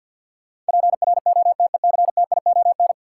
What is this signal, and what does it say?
Morse code, heard as a keyed tone, about 52 wpm, 700 Hz. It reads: PROTECTION
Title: Sentences from Top 1600 Words - New Word Drill - 50wpm